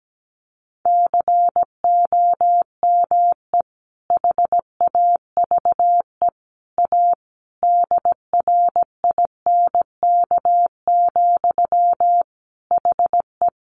Morse code, heard as a keyed tone, about 17 wpm, 700 Hz. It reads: COME HAVE A DRINK, HE
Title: Sentences from Top 1500 Words 17wpm